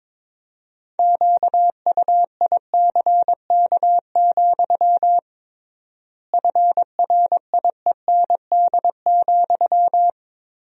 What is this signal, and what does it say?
Morse code, heard as a keyed tone, about 22 wpm, 700 Hz. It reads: QUICK, FRIEND,
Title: 3x Word-Spacing - Sentences from Top 500 Words - 22wpm